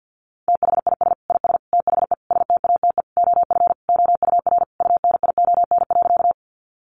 Morse code, heard as a keyed tone, about 50 wpm, 700 Hz. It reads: THIS IS THE STATE OF OUR UNION.
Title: Sentences from Top 1400 Words 50wpm